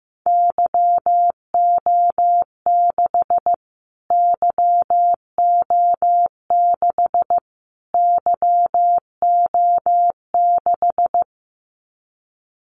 Morse code, heard as a keyed tone, about 15 wpm, 700 Hz. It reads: YO6 YO6 YO6
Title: Callsign Prefixes for Top 25 DXCC - Repeated 3x - 15wpm